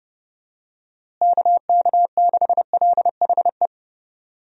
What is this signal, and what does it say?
Morse code, heard as a keyed tone, about 30 wpm, 700 Hz. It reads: KK6LHE
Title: USA Call Signs - 6 Characters - Repeated 2x - 30wpm